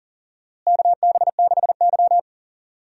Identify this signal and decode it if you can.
Morse code, heard as a keyed tone, about 39 wpm, 700 Hz. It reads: KB6Y
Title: USA Call Signs - 4 Characters - Repeated 3x - 40wpm